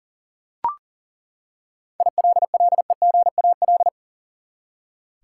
Morse code, heard as a keyed tone, about 40 wpm, 700 Hz. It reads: ILLEGAL
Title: Seven Letter Words - 40wpm